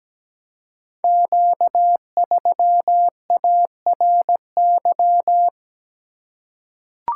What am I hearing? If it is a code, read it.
Morse code, heard as a keyed tone, about 17 wpm, 700 Hz. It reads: Q3ARY
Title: Code Groups - 5 Letter:Numbers - 17wpm